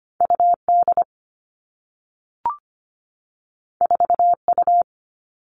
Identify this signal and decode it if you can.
Morse code, heard as a keyed tone, about 25 wpm, 700 Hz. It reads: UD 4U
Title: All Cut Numbers - 2-Digit 25wpm